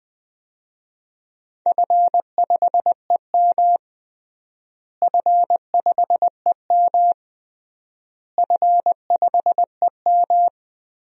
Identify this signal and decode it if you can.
Morse code, heard as a keyed tone, about 21 wpm, 700 Hz. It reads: F5EM F5EM F5EM
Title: DXCC-100 Call Signs - 4 Characters - Repeated 3x - 20wpm